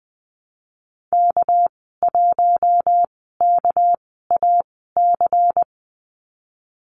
Morse code, heard as a keyed tone, about 20 wpm, spaced as for 15 wpm, 700 Hz. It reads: K1KAC
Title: 219 Lesson 35g - Number 0 - Call Signs 20wpm